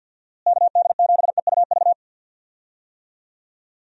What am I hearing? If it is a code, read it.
Morse code, heard as a keyed tone, about 50 wpm, 700 Hz. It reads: KD7EUV
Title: USA Call Signs - 6 Characters - Repeated 3x - 50wpm